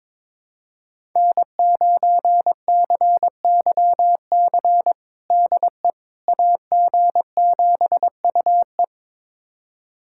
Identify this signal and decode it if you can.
Morse code, heard as a keyed tone, about 22 wpm, 700 Hz. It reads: N9CYC DE AG7UE
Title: USA Callsign DE Callsign - 5 Characters - 22wpm